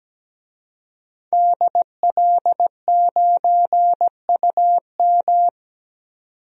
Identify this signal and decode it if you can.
Morse code, heard as a keyed tone, about 17 wpm, 700 Hz. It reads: DL9UM
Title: DXCC-100 Call Signs - 5 Characters - 17wpm